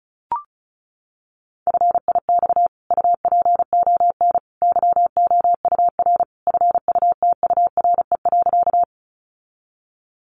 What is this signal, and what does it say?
Morse code, heard as a keyed tone, about 34 wpm, 700 Hz. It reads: FIX UPON YOUR FUTURE.
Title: Sentences from Top 1400 Words 35wpm